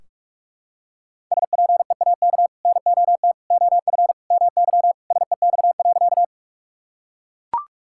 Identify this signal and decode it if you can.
Morse code, heard as a keyed tone, about 44 wpm, 700 Hz. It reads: SPEAK NOT OF MY SEX.
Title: Sentences from Top 1600 Words 45wpm